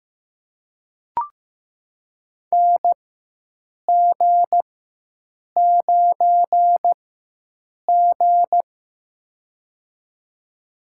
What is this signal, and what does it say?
Morse code, heard as a keyed tone, about 15 wpm, spaced as for 7 wpm, 700 Hz. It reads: NG9G
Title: USA Call Signs - 4 Characters - 4x Character Spacing - 15wpm